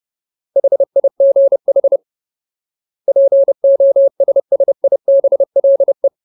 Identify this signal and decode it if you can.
Morse code, heard as a keyed tone, about 30 wpm, 550 Hz. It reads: HIGH POSSIBLE
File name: Ultimate Word List / 30WPM Ultimate